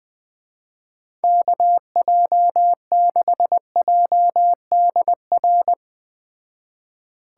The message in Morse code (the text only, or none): KJ6JDR